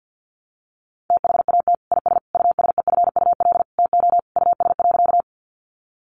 Morse code, heard as a keyed tone, about 50 wpm, 700 Hz. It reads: THAT IS USEFUL TO US.
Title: Sentences from Top 700 Words 50wpm